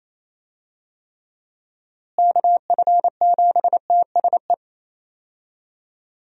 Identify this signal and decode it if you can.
Morse code, heard as a keyed tone, about 28 wpm, 700 Hz. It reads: KF7TSE
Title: USA Call Signs - 6 Characters - Repeated 3x - 28wpm